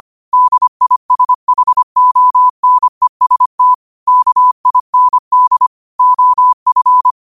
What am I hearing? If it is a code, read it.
Morse code, heard as a keyed tone, about 25 wpm, 1000 Hz. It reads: DISHONEST KIND OF
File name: qotdfiles / 25wpm